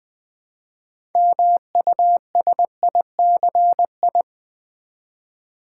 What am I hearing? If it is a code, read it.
Morse code, heard as a keyed tone, about 20 wpm, 700 Hz. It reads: MUSICI
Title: Build A Word - 2 Letters At A Time - 8-Letter Words - 20wpm